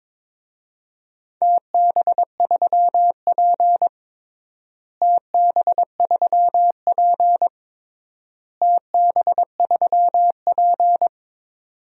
Morse code, heard as a keyed tone, about 22 wpm, 700 Hz. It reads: TB3P TB3P TB3P
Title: DXCC-100 Call Signs - 4 Characters - Repeated 3x - 22wpm